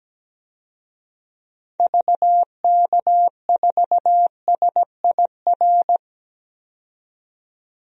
Morse code, heard as a keyed tone, about 17 wpm, 700 Hz. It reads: VK4SIR
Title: DXCC-100 Call Signs - 6 Characters - 17wpm